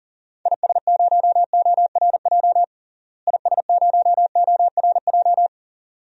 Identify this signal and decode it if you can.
Morse code, heard as a keyed tone, about 40 wpm, 700 Hz. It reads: IS0ORJ IS0ORJ
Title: DXCC-100 Call Signs - 6 Characters - Repeated 2x - 40wpm